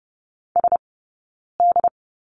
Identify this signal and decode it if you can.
Morse code, heard as a keyed tone, about 30 wpm, 700 Hz. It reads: S D
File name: Lesson 20d - Number 3 - Two Characters 30wpm